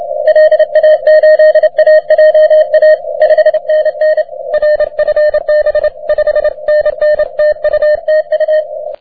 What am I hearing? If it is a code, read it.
Morse code, heard as a keyed tone, about 30 wpm, 620 Hz. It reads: LA8AJA 5NN RFB 5NN TU TU
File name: wp / et3ja 30m cw